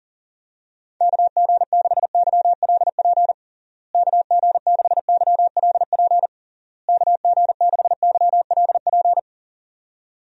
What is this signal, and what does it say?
Morse code, heard as a keyed tone, about 40 wpm, 700 Hz. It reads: KG6YLP KG6YLP KG6YLP
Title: USA Call Signs - 6 Characters - Repeated 3x - 40wpm